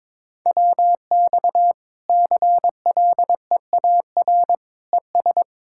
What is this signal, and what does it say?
Morse code, heard as a keyed tone, about 22 wpm, 700 Hz. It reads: WX CLEAR ES
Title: Practice QSO - Exchange 2 - Info - 22wpm